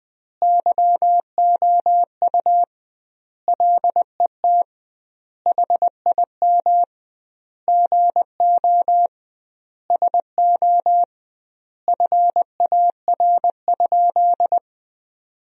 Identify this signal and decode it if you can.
Morse code, heard as a keyed tone, about 20 wpm, 700 Hz. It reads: YOU LET HIM GO SO FAR?
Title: 2x Word-Spacing - Sentences from Top 300 Words - 20wpm